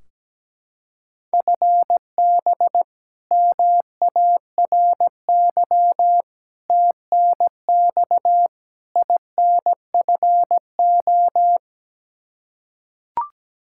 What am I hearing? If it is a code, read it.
Morse code, heard as a keyed tone, about 17 wpm, 700 Hz. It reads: FB MARY TNX INFO
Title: Practice QSO - Exchange 3 - More Info - 17wpm